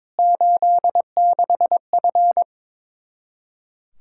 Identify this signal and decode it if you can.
Morse code, heard as a keyed tone, about 22 wpm, 700 Hz. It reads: 86F
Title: K1USN SST QSO Practice 22wpm